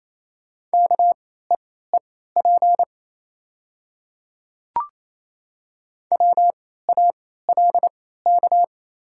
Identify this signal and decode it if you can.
Morse code, heard as a keyed tone, about 28 wpm, spaced as for 16 wpm, 700 Hz. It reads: KEEP WALK
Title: 3x Character Spacing - Top 500 Words - 28wpm